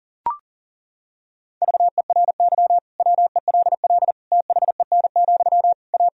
Morse code, heard as a keyed tone, about 40 wpm, 700 Hz. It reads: VERY WELL THEN, A
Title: Sentences from Top 1000 Words - Review Part 3 - 40wpm